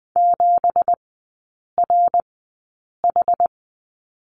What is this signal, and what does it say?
Morse code, heard as a keyed tone, about 20 wpm, spaced as for 8 wpm, 700 Hz. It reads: 7RH
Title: 166 Lesson 27e - Number 7 - Three Characters 20wpm